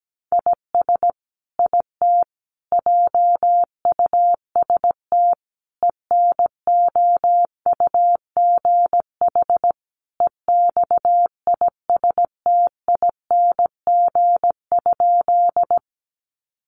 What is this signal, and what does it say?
Morse code, heard as a keyed tone, about 17 wpm, 700 Hz. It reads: IS IT JUST ENOUGH EXISTING?
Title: Sentences from Top 1500 Words 17wpm